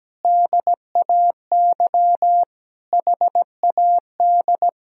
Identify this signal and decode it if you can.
Morse code, heard as a keyed tone, about 17 wpm, 700 Hz. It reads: DAY HAD
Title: Sentences from Top 1000 Words - Review Part 3 - 17wpm